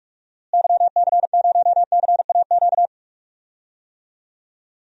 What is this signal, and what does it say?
Morse code, heard as a keyed tone, about 44 wpm, 700 Hz. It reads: YC0CAQ
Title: DXCC-100 Call Signs - 6 Characters - Repeated 2x - 45wpm